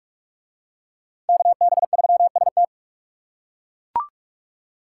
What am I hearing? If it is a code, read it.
Morse code, heard as a keyed tone, about 40 wpm, 700 Hz. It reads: KB3ST